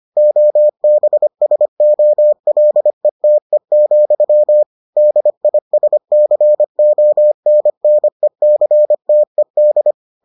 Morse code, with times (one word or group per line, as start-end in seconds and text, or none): 0.2-4.6 OBSOLETE,
5.0-9.9 DISCONNECTED